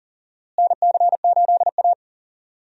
Morse code, heard as a keyed tone, about 37 wpm, 700 Hz. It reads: NC8A
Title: USA Call Signs - 4 Characters - Repeated 3x - 40wpm